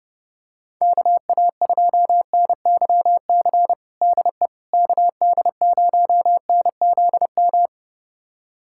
Morse code, heard as a keyed tone, about 30 wpm, 700 Hz. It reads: KA2NYC DE KD0NZM
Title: USA Callsign DE Callsign - 6 Characters - 30wpm